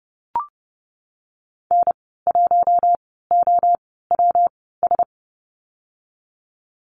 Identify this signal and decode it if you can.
Morse code, heard as a keyed tone, about 30 wpm, 700 Hz. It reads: N 1 O W S
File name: Lesson 19g - Letter w - Call Signs 30wpm